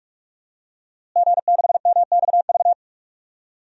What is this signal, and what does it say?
Morse code, heard as a keyed tone, about 41 wpm, 700 Hz. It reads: G6MXV
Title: DXCC-100 Call Signs - 5 Characters - Repeated 3x - 45wpm